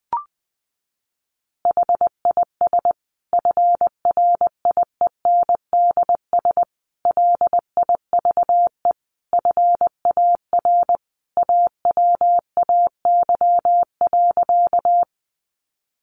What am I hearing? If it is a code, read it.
Morse code, heard as a keyed tone, about 20 wpm, 700 Hz. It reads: HIS FRIENDS LIVE FAR AWAY.
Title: Additional Easy Phrases 20wpm